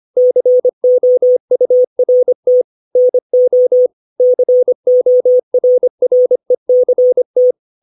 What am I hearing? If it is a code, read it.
Morse code, heard as a keyed tone, about 25 wpm, 500 Hz. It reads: COURT NO CORRECT
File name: 632 words / 632 Words 25WPM